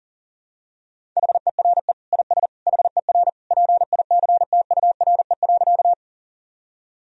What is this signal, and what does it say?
Morse code, heard as a keyed tone, about 37 wpm, 700 Hz. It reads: HERE IS HER PICTURE.
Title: Sentences from Top 1000 Words - Review Part 1 - 40wpm